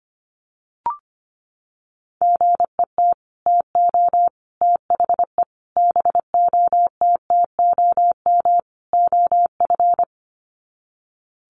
Morse code, heard as a keyed tone, about 25 wpm, 700 Hz. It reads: GET TO THE BOTTOM OF
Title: Sets of 5 Words Encore 25wpm